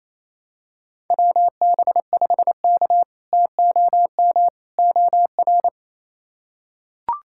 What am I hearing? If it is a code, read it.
Morse code, heard as a keyed tone, about 28 wpm, 700 Hz. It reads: WB5K TOM OR
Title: K1USN SST QSO Practice - 28wpm